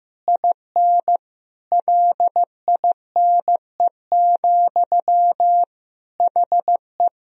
Morse code, heard as a keyed tone, about 15 wpm, 700 Hz. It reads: IN LINE, HE
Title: Sentences from Top 1000 Words - Review Part 1 - 15wpm